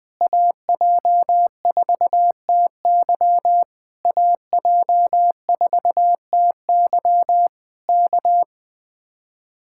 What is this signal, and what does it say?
Morse code, heard as a keyed tone, about 20 wpm, 700 Hz. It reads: AJ4TY AJ4TY K
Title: Speed-Racing Common QSO Phrases Encore - 17wpm